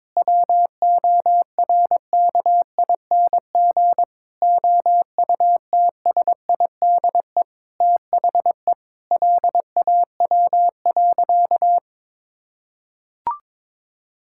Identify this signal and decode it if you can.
Morse code, heard as a keyed tone, about 22 wpm, 700 Hz. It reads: WORKING OUTSIDE THE LAW.
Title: Sentences from Top 1000 Words - Review Part 6 - 22wpm